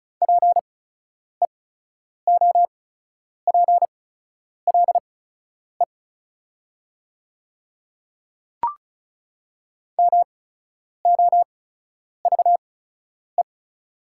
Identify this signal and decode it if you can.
Morse code, heard as a keyed tone, about 36 wpm, spaced as for 10 wpm, 690 Hz. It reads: PEOPLE MOVE
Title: 8x Character Spacing - Top 300 Words - 35wpm